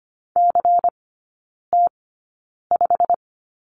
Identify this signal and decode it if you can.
Morse code, heard as a keyed tone, about 25 wpm, spaced as for 9 wpm, 700 Hz. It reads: CT5
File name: course / 25:10wpm / 093 Lesson 16e - Letter c - Three Characters 25wpm